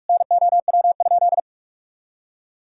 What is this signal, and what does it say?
Morse code, heard as a keyed tone, about 46 wpm, 700 Hz. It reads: NOW?
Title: Sentences from Top 1000 Words - Review Part 5 - 45wpm